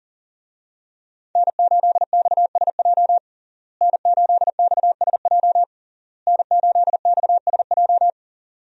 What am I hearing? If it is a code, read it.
Morse code, heard as a keyed tone, about 40 wpm, 700 Hz. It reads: N8XSJ N8XSJ N8XSJ
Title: USA Call Signs - 5 Characters - Repeated 3x - 40wpm